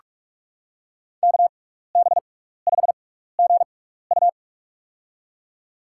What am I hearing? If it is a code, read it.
Morse code, heard as a keyed tone, about 47 wpm, spaced as for 16 wpm, 700 Hz. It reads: KB5GU
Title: USA Call Signs - 5 Characters - 6x Character Spacing - 45wpm